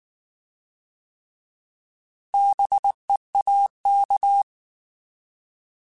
Morse code, heard as a keyed tone, about 19 wpm, 800 Hz. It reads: BEAK
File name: Alphabetical / Alphabetical mp3s / Bs @18